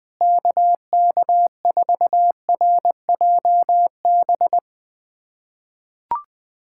Morse code, heard as a keyed tone, about 20 wpm, 700 Hz. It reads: KK4RJB
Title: USA Callsign DE Callsign - 6 Characters - 20wpm